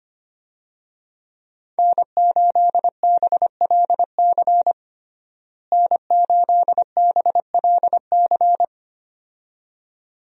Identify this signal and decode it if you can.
Morse code, heard as a keyed tone, about 25 wpm, 700 Hz. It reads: N8BLC N8BLC
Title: USA Call Signs - 5 Characters - Repeated 2x - 25wpm